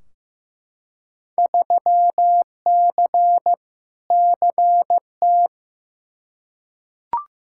Morse code, heard as a keyed tone, about 15 wpm, 700 Hz. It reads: 3C CT
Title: Field Day Exchanges Encore - 15wpm